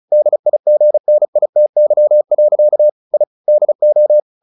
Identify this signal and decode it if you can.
Morse code, heard as a keyed tone, about 35 wpm, 600 Hz. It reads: DIGNITY. I DO
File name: the old man and the sea 2 35wpm